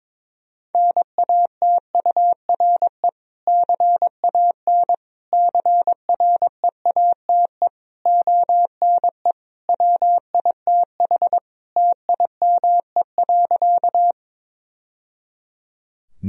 Morse code, heard as a keyed tone, about 22 wpm, 700 Hz. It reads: NATURE CAN CREATE ONE WITH TIME.
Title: Sentences from Top 1000 Words - Review Part 3 - 22wpm